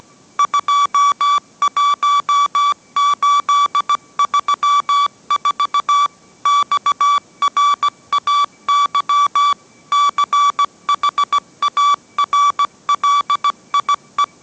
Morse code, heard as a keyed tone, about 17 wpm, 1150 Hz. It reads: 21834 XRAY CHARLIE